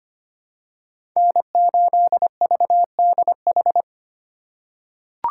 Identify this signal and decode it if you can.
Morse code, heard as a keyed tone, about 25 wpm, 700 Hz. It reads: N8VDH
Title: USA Call Signs - 5 Characters - 3x Character Spacing - 25wpm